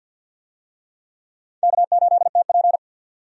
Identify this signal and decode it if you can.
Morse code, heard as a keyed tone, about 52 wpm, 700 Hz. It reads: K8TP
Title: USA Call Signs - 4 Characters - 50wpm